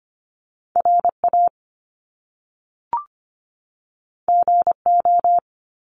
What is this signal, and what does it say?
Morse code, heard as a keyed tone, about 25 wpm, 700 Hz. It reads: RA GO
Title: Top Two Letter Combinations in Words 25wpm